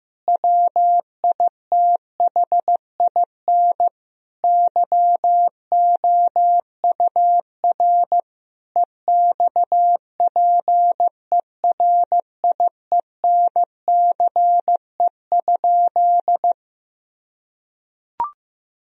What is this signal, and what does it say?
Morse code, heard as a keyed tone, about 16 wpm, 700 Hz. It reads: WITHIN YOUR EXPERIENCE?
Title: Sentences from Top 1000 Words - Review Part 4 - 15wpm